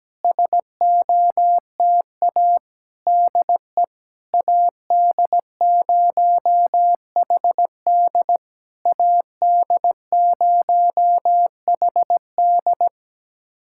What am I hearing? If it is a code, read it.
Morse code, heard as a keyed tone, about 17 wpm, 700 Hz. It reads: SOTA DE AD0HD AD0HD
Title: SOTA QSO Practice 17wpm